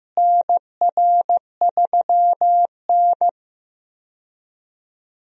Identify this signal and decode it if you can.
Morse code, heard as a keyed tone, about 15 wpm, 700 Hz. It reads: NR3N
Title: USA Call Signs - 4 Characters - Repeated 2x - 15wpm